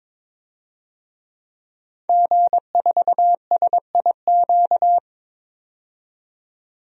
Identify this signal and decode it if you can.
Morse code, heard as a keyed tone, about 22 wpm, 700 Hz. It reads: G4SIQ
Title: DXCC-100 Call Signs - 5 Characters - 22wpm